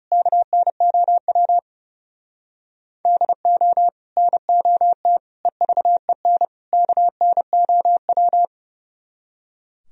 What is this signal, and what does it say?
Morse code, heard as a keyed tone, about 40 wpm, 700 Hz. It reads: KNOW DO NOT EVEN KNOW